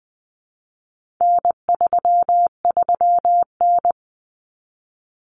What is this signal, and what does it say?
Morse code, heard as a keyed tone, about 20 wpm, 700 Hz. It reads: N33N
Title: Common Cut Numbers - 4-Digit 20wpm